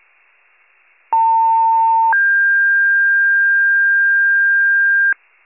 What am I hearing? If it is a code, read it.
Two-tone page, heard as a one-second tone, then a three-second tone, about 900 then 1600 Hz, with noise. It is heard from 1.1 to 5.1 s.